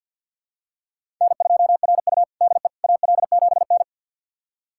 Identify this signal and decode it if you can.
Morse code, heard as a keyed tone, about 51 wpm, 700 Hz. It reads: N2RU DE AL7N
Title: USA Callsign DE Callsign - 4 Characters - 50wpm